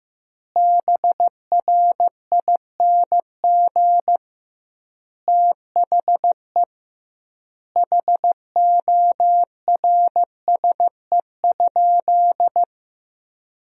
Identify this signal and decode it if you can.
Morse code, heard as a keyed tone, about 15 wpm, 700 Hz. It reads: BRING THE HORSE?
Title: 2x Word-Spacing - Sentences from Top 500 Words - 15wpm